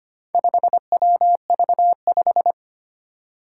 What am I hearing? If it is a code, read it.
Morse code, heard as a keyed tone, about 25 wpm, 700 Hz. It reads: 5WV5
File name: practice / 25wpm / Four Letter-Number Combinations - 25wpm